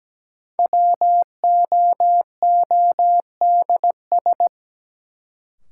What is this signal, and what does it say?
Morse code, heard as a keyed tone, about 17 wpm, 700 Hz. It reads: WOODS
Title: Sentences from Top 1000 Words - Review Part 2 - New Word Drill - 17wpm